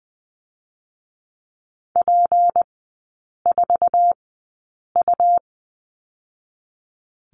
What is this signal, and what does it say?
Morse code, heard as a keyed tone, about 20 wpm, 700 Hz. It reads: P 4 U
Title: 184 Lesson 30e - Letter b - Three Characters 20wpm